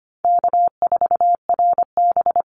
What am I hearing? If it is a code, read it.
Morse code, heard as a keyed tone, about 25 wpm, 700 Hz. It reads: K4RB
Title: Easy Call Signs Repeated 3x - 25wpm